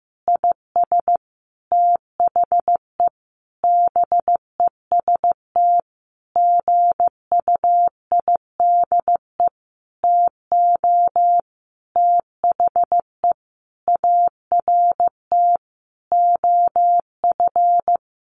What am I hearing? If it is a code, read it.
Morse code, heard as a keyed tone, about 15 wpm, 700 Hz. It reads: IS THE BEST GUIDE TO THE ART OF
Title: Sentences from Top 1200 Words 15wpm